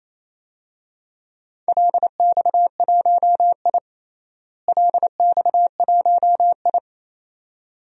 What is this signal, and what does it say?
Morse code, heard as a keyed tone, about 28 wpm, 700 Hz. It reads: LX1I LX1I
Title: DXCC-100 Call Signs - 4 Characters - Repeated 2x - 28wpm